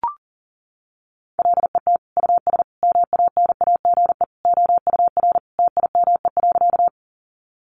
Morse code, heard as a keyed tone, about 40 wpm, 700 Hz. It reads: LET US MANAGE OUR TIME.
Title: Sentences from Top 1400 Words 40wpm